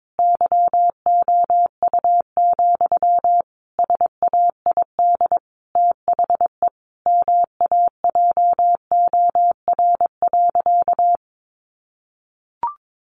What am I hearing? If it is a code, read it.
Morse code, heard as a keyed tone, about 22 wpm, 700 Hz. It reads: YOU, SAID THE MAJOR.
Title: Sentences from Top 600 Words 22wpm